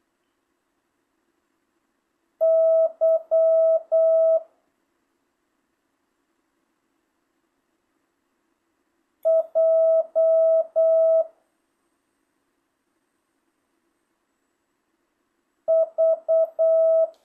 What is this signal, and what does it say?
Morse code, heard as a keyed tone, about 8 wpm, 650 Hz. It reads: Y J V
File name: jh8cbh / 2025cw